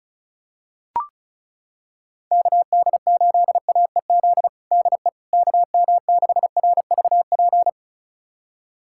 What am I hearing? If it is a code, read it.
Morse code, heard as a keyed tone, about 36 wpm, 700 Hz. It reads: KD8AEZ DE KM6RVP
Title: USA Callsign DE Callsign - 6 Characters - 35wpm